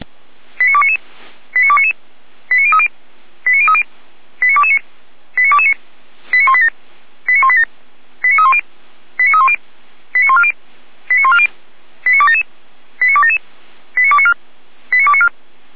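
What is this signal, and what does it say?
Five-tone selective call (ZVEI), heard as quick runs of five tones, tones of about 80 ms. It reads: 89299 89299 89030 89030 89229 89229 89187 89187 89310 89310 89140 89140 89277 89277 89294 89294
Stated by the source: ZVEI five-tone selective call received by SDR.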